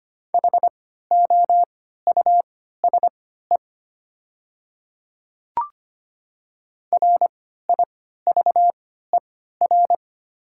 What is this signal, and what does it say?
Morse code, heard as a keyed tone, about 25 wpm, spaced as for 14 wpm, 690 Hz. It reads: HOUSE RIVER
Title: 3x Character Spacing - Top 300 Words - 25wpm